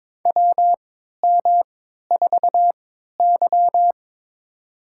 Morse code, heard as a keyed tone, about 22 wpm, spaced as for 12 wpm, 690 Hz. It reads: WM4Y
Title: USA Call Signs - 4 Characters - 3x Character Spacing - 22wpm